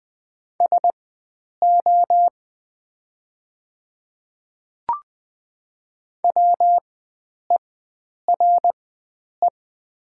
Morse code, heard as a keyed tone, about 20 wpm, spaced as for 9 wpm, 700 Hz. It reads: SO WERE
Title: 4x Character Spacing - Top 100 Words - 20wpm